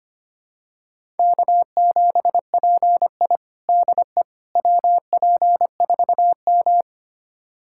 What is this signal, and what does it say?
Morse code, heard as a keyed tone, about 25 wpm, 700 Hz. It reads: K7PI DE WP4M
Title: USA Callsign DE Callsign - 4 Characters - 25wpm